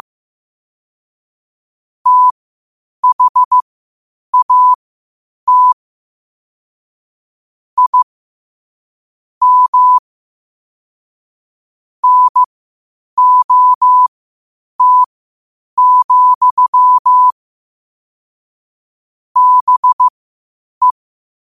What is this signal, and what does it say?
Morse code, heard as a keyed tone, about 15 wpm, spaced as for 8 wpm, 1000 Hz. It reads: THAT I M NOT, BE
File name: qotdfiles / 8wpm